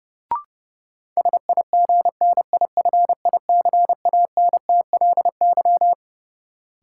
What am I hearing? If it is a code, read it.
Morse code, heard as a keyed tone, about 30 wpm, 700 Hz. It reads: SIGNIFICANTLY
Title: Sentences from Top 1500 Words - New Word Drill - 30wpm